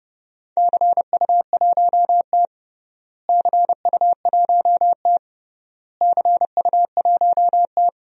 Morse code, heard as a keyed tone, about 30 wpm, 700 Hz. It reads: CU1T CU1T CU1T
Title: DXCC-100 Call Signs - 4 Characters - Repeated 3x - 30wpm